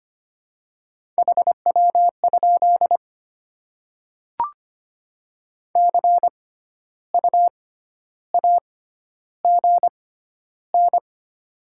Morse code, heard as a keyed tone, about 25 wpm, spaced as for 12 wpm, 700 Hz. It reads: HW? CU AGN
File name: Top 20 QSO Elements - 6x Character Spacing - 25wpm